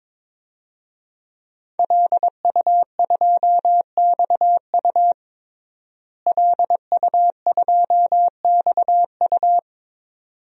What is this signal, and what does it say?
Morse code, heard as a keyed tone, about 21 wpm, 700 Hz. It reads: LU2XU LU2XU
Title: DXCC-100 Call Signs - 5 Characters - Repeated 2x - 22wpm